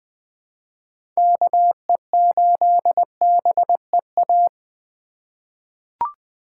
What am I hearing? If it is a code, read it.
Morse code, heard as a keyed tone, about 20 wpm, 700 Hz. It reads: KE8BEA
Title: USA Call Signs - 6 Characters - 2x Character Spacing - 20wpm